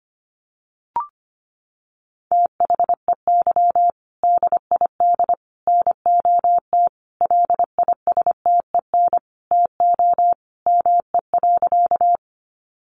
Morse code, heard as a keyed tone, about 25 wpm, 700 Hz. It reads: THEY DID NOT LISTEN TO ME.
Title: Sentences from Top 600 Words 25wpm